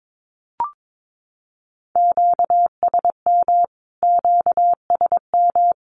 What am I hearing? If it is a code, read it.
Morse code, heard as a keyed tone, about 22 wpm, 700 Hz. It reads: QSM QSM
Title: Q-Codes Repeated 3x 22wpm